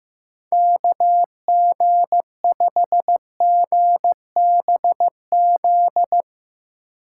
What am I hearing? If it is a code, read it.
Morse code, heard as a keyed tone, about 16 wpm, 700 Hz. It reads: KG5GBZ
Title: Speed Racing - USA 2x3 Call Signs 15wpm